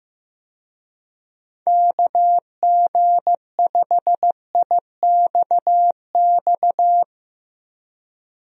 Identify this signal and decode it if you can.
Morse code, heard as a keyed tone, about 15 wpm, 700 Hz. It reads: KG5IXX
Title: USA Call Signs - 6 Characters - 15wpm